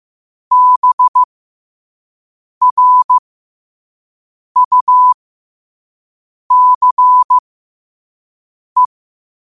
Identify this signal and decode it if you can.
Morse code, heard as a keyed tone, about 15 wpm, spaced as for 5 wpm, 1000 Hz. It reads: BRUCE
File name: qotdfiles / 5wpm